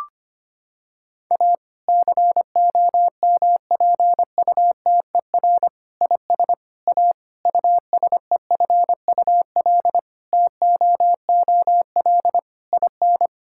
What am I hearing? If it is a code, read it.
Morse code, heard as a keyed tone, about 25 wpm, 700 Hz. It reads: A COMPUTER IS A USEFUL TOOL IN
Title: Sentences from Top 1000 Words - Review Part 6 - 25wpm